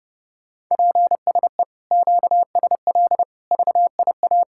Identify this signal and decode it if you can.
Morse code, heard as a keyed tone, about 34 wpm, 700 Hz. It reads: PSE QSL VIA
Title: Speed-Racing Common QSO Phrases Encore - 30wpm